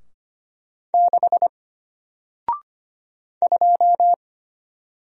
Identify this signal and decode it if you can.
Morse code, heard as a keyed tone, about 25 wpm, 700 Hz. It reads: 6 2
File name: Single-Letter-Number-25wpm